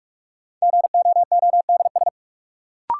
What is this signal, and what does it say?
Morse code, heard as a keyed tone, about 44 wpm, 700 Hz. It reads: GOODS
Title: Sentences from Top 1000 Words - Review Part 6 - New Word Drill - 45wpm